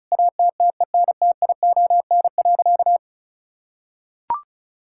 Morse code, heard as a keyed tone, about 34 wpm, 700 Hz. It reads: ATTENTION.
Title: Sentences from Top 1000 Words - Review Part 6 - 35wpm